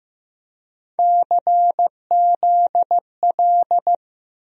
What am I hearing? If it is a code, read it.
Morse code, heard as a keyed tone, about 15 wpm, 700 Hz. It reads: CZL
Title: USA Call Sign Postfixes - 3 Letters 15wpm